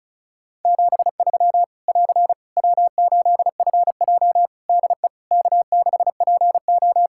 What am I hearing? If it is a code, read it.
Morse code, heard as a keyed tone, about 36 wpm, 700 Hz. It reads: 73 <AR> W8FJ DE K6PO
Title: Practice QSO - Exchange 4 - Ending - 35wpm